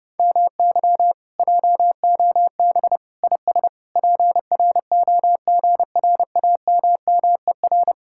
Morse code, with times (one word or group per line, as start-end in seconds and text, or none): 0.2-1.1 MY
1.4-3.0 JOB
3.2-3.7 IS
4.0-7.9 PROGRAMMER